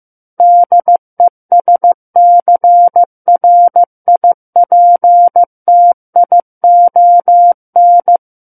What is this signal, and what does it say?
Morse code, heard as a keyed tone, about 15 wpm, 700 Hz. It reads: DESCRIPTION
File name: Book One / 06-Chapter 6 - 15 WPM